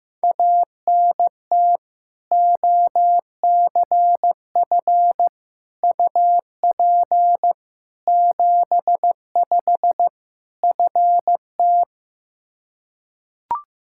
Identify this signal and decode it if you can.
Morse code, heard as a keyed tone, about 15 wpm, 700 Hz. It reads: ANT OCF UP 75 FT
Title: Practice QSO - Exchange 2 - Info - 15wpm